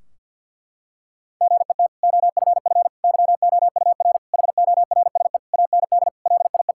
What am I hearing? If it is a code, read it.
Morse code, heard as a keyed tone, about 51 wpm, 700 Hz. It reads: GET OFF YOUR HORSE AND LIE
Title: Sentences from Top 1600 Words 50wpm